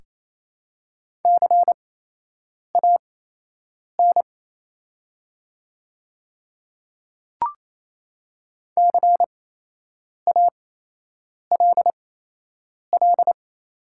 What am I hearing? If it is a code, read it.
Morse code, heard as a keyed tone, about 28 wpm, spaced as for 8 wpm, 700 Hz. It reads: CAN CALL
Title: 8x Character Spacing - Top 100 Words - 28wpm